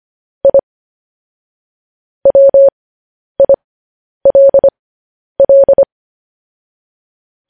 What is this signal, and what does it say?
Morse code, heard as a keyed tone, about 25 wpm, spaced as for 10 wpm, 550 Hz. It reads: I WILL
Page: news headlines in morse code at 10 wpm